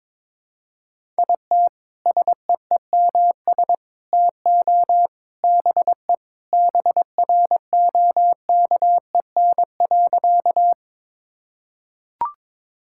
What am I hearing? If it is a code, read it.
Morse code, heard as a keyed tone, about 22 wpm, 700 Hz. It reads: IT SEEMS TO BE BROKEN.
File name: practice / 22wpm / Sentences with 3-Word Phrases - 22wpm